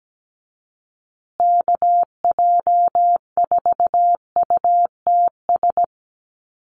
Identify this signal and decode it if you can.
Morse code, heard as a keyed tone, about 17 wpm, 700 Hz. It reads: KJ4UTS
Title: USA Call Signs Repeated 3x 17wpm